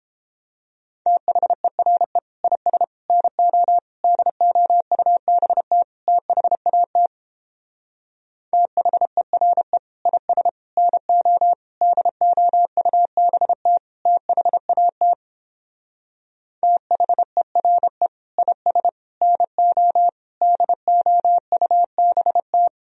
Morse code, heard as a keyed tone, about 29 wpm, 700 Hz. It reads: THERE IS NO DOUBT THAT THERE IS NO DOUBT THAT THERE IS NO DOUBT